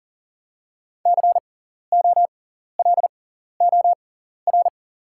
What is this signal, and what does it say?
Morse code, heard as a keyed tone, about 40 wpm, spaced as for 14 wpm, 700 Hz. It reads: COLOR
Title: 6x Character Spacing - Top 300 Words - 40wpm